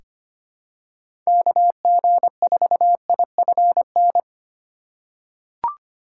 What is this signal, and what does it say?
Morse code, heard as a keyed tone, about 25 wpm, 700 Hz. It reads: KG4IFN
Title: USA Call Signs - 6 Characters - 3x Character Spacing - 25wpm